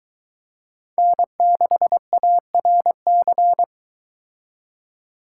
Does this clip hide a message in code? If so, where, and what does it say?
Yes: Morse code from 1.0 to 3.6 s, heard as a keyed tone, about 22 wpm, 700 Hz. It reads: N6ARC